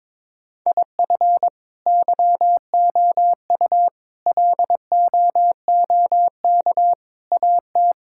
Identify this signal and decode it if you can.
Morse code, heard as a keyed tone, about 22 wpm, 700 Hz. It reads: IF YOU LOOK AT
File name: practice / 22wpm / Build A Phrase - 5-Word Phrases - 22wpm